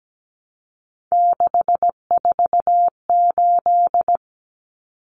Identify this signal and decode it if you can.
Morse code, heard as a keyed tone, about 17 wpm, 700 Hz. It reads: 648
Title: Numbers - 3-Digit 17wpm